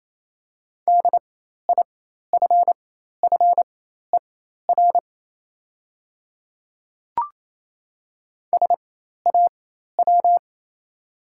Morse code, heard as a keyed tone, about 28 wpm, spaced as for 13 wpm, 700 Hz. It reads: DIFFER SAW
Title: 4x Character Spacing - Top 500 Words - 28wpm